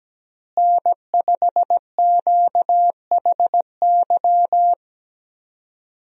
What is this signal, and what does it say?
Morse code, heard as a keyed tone, about 17 wpm, 700 Hz. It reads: N5QHY
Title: USA Call Signs - 5 Characters - Repeated 3x - 17wpm